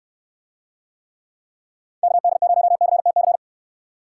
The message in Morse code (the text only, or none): KD0CTC